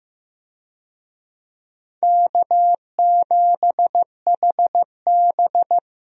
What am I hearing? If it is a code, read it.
Morse code, heard as a keyed tone, about 15 wpm, 700 Hz. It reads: K7HB
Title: USA Call Signs - 4 Characters - Repeated 3x - 15wpm